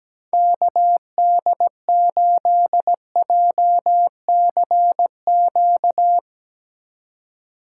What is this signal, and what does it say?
Morse code, heard as a keyed tone, about 17 wpm, 700 Hz. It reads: KD8JCQ
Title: USA Call Signs - 6 Characters - 4x Character Spacing - 17wpm